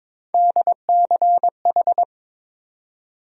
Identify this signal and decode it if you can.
Morse code, heard as a keyed tone, about 22 wpm, 700 Hz. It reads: DCH
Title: USA Call Sign Postfixes - 3 Letters 22wpm